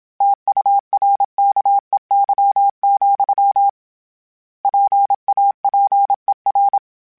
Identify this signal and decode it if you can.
Morse code, heard as a keyed tone, about 27 wpm, 800 Hz. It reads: TURKEY, PAPER
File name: Misc / Misc-mp3s / Word Pairs @25